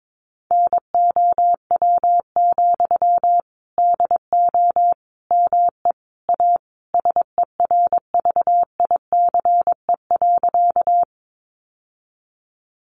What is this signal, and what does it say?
Morse code, heard as a keyed tone, about 22 wpm, 700 Hz. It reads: NOW, DO ME A SERVICE.
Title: Sentences from Top 700 Words 22wpm